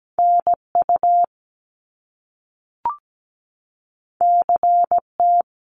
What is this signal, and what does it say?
Morse code, heard as a keyed tone, about 17 wpm, 700 Hz. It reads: NU CT
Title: Top Two Letter Combinations in Words 17wpm